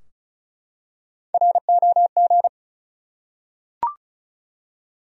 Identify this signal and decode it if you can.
Morse code, heard as a keyed tone, about 34 wpm, 700 Hz. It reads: ROG
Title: USA Common Names - Max 4 Letters - 35wpm